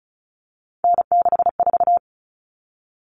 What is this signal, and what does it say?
Morse code, heard as a keyed tone, about 34 wpm, 700 Hz. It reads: N64
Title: Common Cut Numbers - 3-Digit 35wpm